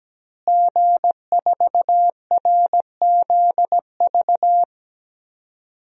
Morse code, heard as a keyed tone, about 17 wpm, 700 Hz. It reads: G4RZV
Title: DXCC-100 Call Signs - 5 Characters - Repeated 3x - 17wpm